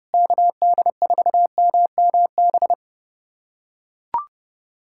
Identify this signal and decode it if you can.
Morse code, heard as a keyed tone, about 30 wpm, 700 Hz. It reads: KD4MMB
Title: USA Callsign DE Callsign 30wpm